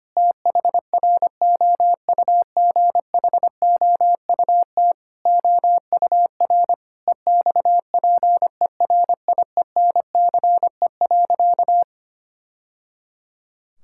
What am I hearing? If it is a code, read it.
Morse code, heard as a keyed tone, about 25 wpm, 700 Hz. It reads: THROUGHOUT OUR EXPERIENCE.
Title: Sentences from Top 1000 Words - Review Part 5 - 25wpm